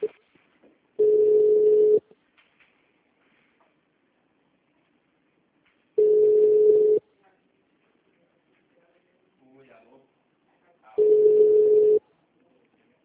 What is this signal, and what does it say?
Ringback tone (Europe, 425 Hz), with speech.
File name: out- -2007- - - .76230